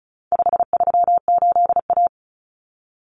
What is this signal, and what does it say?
Morse code, heard as a keyed tone, about 34 wpm, 700 Hz. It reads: 538A
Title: Common Cut Numbers - 4-Digit 35wpm